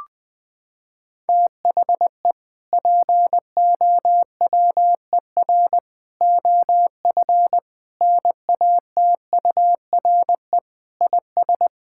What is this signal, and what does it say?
Morse code, heard as a keyed tone, about 20 wpm, 700 Hz. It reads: THE POWER OF NATURE IS
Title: Sentences from Top 1000 Words - Review Part 5 - 20wpm